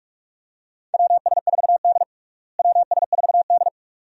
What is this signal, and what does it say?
Morse code, heard as a keyed tone, about 46 wpm, 700 Hz. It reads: WS4D WS4D
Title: USA Call Signs - 4 Characters - Repeated 2x - 45wpm